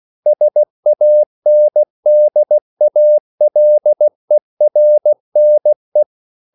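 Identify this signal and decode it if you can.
Morse code, heard as a keyed tone, about 16 wpm, 600 Hz. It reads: SANDALERNE
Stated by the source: Slow (QRS) Morse code at 16 wpm, from CW med Gnister.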